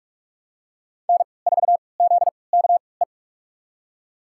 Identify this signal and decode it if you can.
Morse code, heard as a keyed tone, about 42 wpm, spaced as for 25 wpm, 700 Hz. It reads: N4ZKE
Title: USA Call Signs - 3x Character Spacing - 45wpm